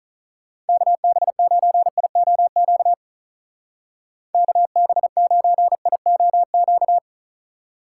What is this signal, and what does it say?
Morse code, heard as a keyed tone, about 45 wpm, 700 Hz. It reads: KB9IOQ KB9IOQ